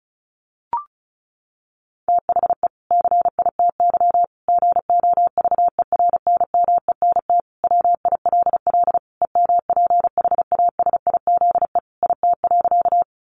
Morse code, heard as a keyed tone, about 34 wpm, 700 Hz. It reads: THE CITY GOVERNMENT WILL EMPHASIZE IT.
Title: Sentences from Top 1500 Words 35wpm